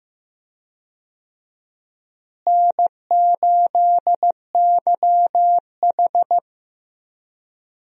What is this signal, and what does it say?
Morse code, heard as a keyed tone, about 15 wpm, 700 Hz. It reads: N8YH
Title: USA Call Signs - 4 Characters - Repeated 2x - 15wpm